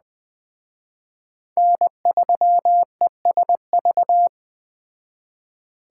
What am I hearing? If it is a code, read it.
Morse code, heard as a keyed tone, about 20 wpm, 700 Hz. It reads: N3ESV